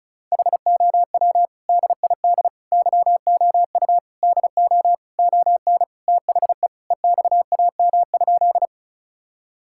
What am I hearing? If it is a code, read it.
Morse code, heard as a keyed tone, about 36 wpm, 700 Hz. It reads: HOW DID YOU DO ON THE EXAM?
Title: Sentences from Top 1000 Words - Review Part 5 - 35wpm